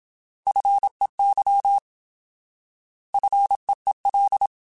Morse code, heard as a keyed tone, about 27 wpm, 800 Hz. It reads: FEY FEEL
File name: Alphabetical / Alphabetical mp3s / Fs @25